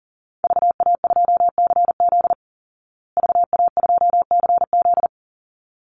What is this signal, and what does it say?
Morse code, heard as a keyed tone, about 40 wpm, 700 Hz. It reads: VA2CZ VA2CZ
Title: Call Signs Repeated 3x - 40wpm